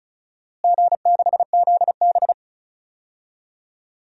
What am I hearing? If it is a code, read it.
Morse code, heard as a keyed tone, about 36 wpm, 700 Hz. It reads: G6ZB
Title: DXCC-100 Call Signs - 4 Characters - Repeated 3x - 35wpm